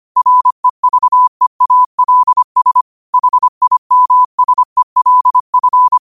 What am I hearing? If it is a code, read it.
Morse code, heard as a keyed tone, about 25 wpm, 1000 Hz. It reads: REVEALS HIMSELF
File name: qotdfiles / 25wpm